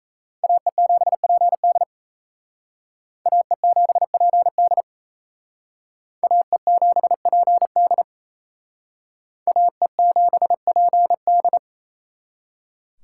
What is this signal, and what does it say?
Morse code, heard as a keyed tone, about 36 wpm, 700 Hz. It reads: AE7PD AE7PD AE7PD AE7PD